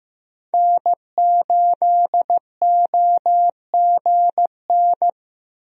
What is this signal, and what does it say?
Morse code, heard as a keyed tone, about 15 wpm, 700 Hz. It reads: N8OGN
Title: USA Call Signs - 5 Characters - Repeated 2x - 15wpm